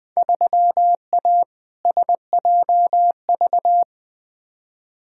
Morse code, heard as a keyed tone, about 21 wpm, 700 Hz. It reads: 3A SJV
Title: Field Day Exchanges Encore - 20wpm